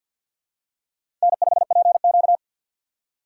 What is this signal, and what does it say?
Morse code, heard as a keyed tone, about 52 wpm, 700 Hz. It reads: N5PQ
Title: USA Call Signs - 2x Character Spacing - 50wpm